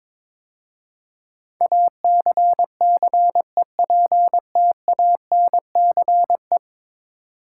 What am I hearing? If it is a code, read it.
Morse code, heard as a keyed tone, about 22 wpm, 700 Hz. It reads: ACCEPTANCE
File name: practice / 22wpm / 10 Letter Words - First 3 Letters Given - 22wpm